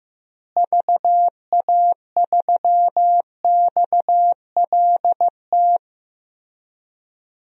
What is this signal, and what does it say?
Morse code, heard as a keyed tone, about 15 wpm, 700 Hz. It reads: VA3XLT
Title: DXCC-100 Call Signs - 6 Characters - Repeated 3x - 15wpm